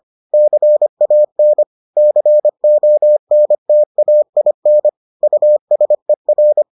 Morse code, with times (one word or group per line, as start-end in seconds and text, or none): 0.3-1.6 CAN
2.0-4.9 CONTAIN
5.2-6.6 USER